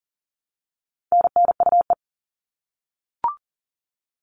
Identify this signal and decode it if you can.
Morse code, heard as a keyed tone, about 40 wpm, 700 Hz. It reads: NNUE
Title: All Cut Numbers - 4-Digit 40wpm